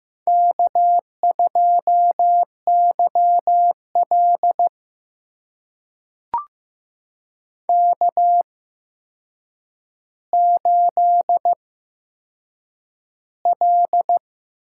Morse code, heard as a keyed tone, about 15 wpm, spaced as for 7 wpm, 700 Hz. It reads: K2YL K8L
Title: USA Call Signs - 4 Characters - 8x Character Spacing - 15wpm